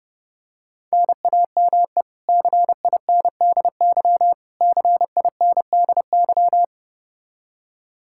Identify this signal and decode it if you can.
Morse code, heard as a keyed tone, about 32 wpm, 700 Hz. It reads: NAME CINDY CINDY
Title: K1USN SST QSO Practice 30wpm